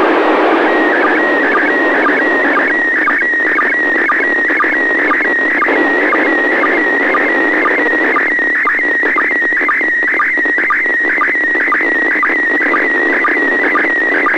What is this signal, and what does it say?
SSTV (slow-scan television), PD 120.
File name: ISS 21h24 UTC 145MHz800 pic02